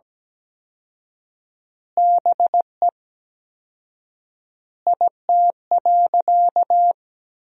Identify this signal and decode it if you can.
Morse code, heard as a keyed tone, about 17 wpm, 700 Hz. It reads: BE IT.
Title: 4x Word-Spacing - Sentences from Top 100 Words - 17wpm